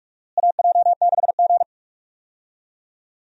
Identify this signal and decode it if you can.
Morse code, heard as a keyed tone, about 44 wpm, 700 Hz. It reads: AJ6G
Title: USA Call Signs - 4 Characters - Repeated 2x - 45wpm